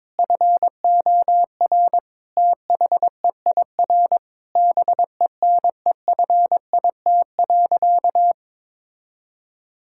Morse code, heard as a keyed tone, about 21 wpm, 700 Hz. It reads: FOR THEIR BENEFIT.
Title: Sentences from Top 1000 Words - Review Part 6 - 22wpm